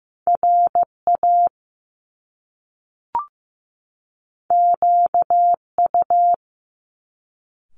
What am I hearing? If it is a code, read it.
Morse code, heard as a keyed tone, about 15 wpm, 700 Hz. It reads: RA QU
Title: Top Two Letter Combinations in Words 15wpm